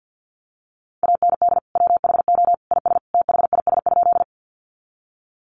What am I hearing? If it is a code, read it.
Morse code, heard as a keyed tone, about 50 wpm, 700 Hz. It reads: AND WHO IS THIS?
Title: Sentences from Top 100 Words 50wpm